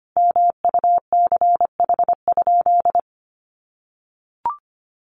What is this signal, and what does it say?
Morse code, heard as a keyed tone, about 25 wpm, 700 Hz. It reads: MUCH?
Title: Sentences from Top 200 Words 25wpm